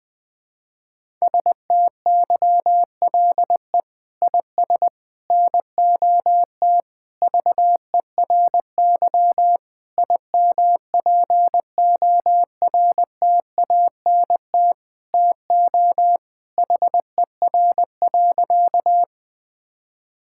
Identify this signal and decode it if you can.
Morse code, heard as a keyed tone, about 20 wpm, 700 Hz. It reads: STYLE IS NOT VERY IMPORTANT TO HER.
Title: Sentences from Top 1000 Words - Review Part 5 - 20wpm